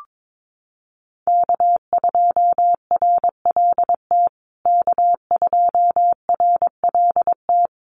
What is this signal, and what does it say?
Morse code, heard as a keyed tone, about 22 wpm, 700 Hz. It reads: K2RLT K2RLT
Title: USA Call Signs Repeated 2x 22wpm